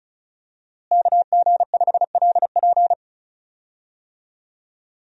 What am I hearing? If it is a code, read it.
Morse code, heard as a keyed tone, about 36 wpm, 700 Hz. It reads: KG5LP
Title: USA Call Signs - 5 Characters - Repeated 3x - 35wpm